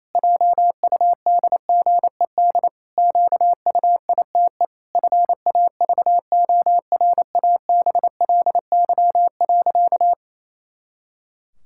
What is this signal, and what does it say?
Morse code, heard as a keyed tone, about 28 wpm, 700 Hz. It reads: JUDGED QUITE FAVORABLY.
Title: Pangram Challenge v2 - 28wpm